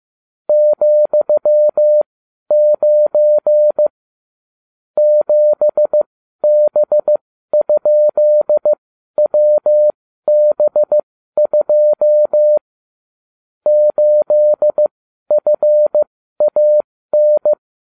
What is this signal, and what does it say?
Morse code, heard as a keyed tone, about 15 wpm, spaced as for 12 wpm, 600 Hz. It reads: ,9 7B?WB2 8FAN
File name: cw / ubacw34 3